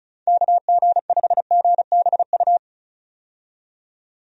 Morse code, heard as a keyed tone, about 36 wpm, 700 Hz. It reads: KG5GBU
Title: USA Call Signs - 6 Characters - 35wpm